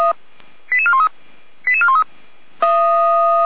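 Five-tone selective call (ZVEI), heard as quick runs of five tones, tones of about 70 ms. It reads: 88413 88413
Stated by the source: ZVEI five-tone call received by SDR.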